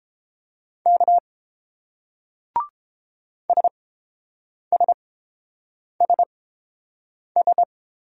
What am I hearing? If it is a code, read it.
Morse code, heard as a keyed tone, about 30 wpm, spaced as for 7 wpm, 700 Hz. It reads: K SSSS